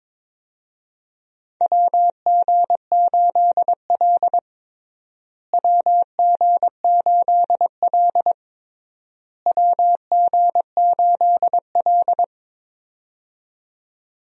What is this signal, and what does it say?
Morse code, heard as a keyed tone, about 21 wpm, 700 Hz. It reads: WG8L WG8L WG8L
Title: USA Call Signs - 4 Characters - Repeated 3x - 22wpm